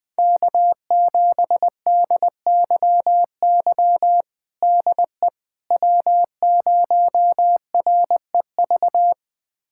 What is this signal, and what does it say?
Morse code, heard as a keyed tone, about 20 wpm, 700 Hz. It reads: K7DYY DE W0REV
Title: USA Callsign DE Callsign - 5 Characters - 20wpm